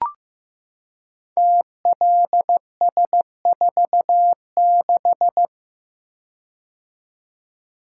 Morse code, heard as a keyed tone, about 15 wpm, 700 Hz. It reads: TLS46
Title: Code Groups - 5 Letter:Numbers - 15wpm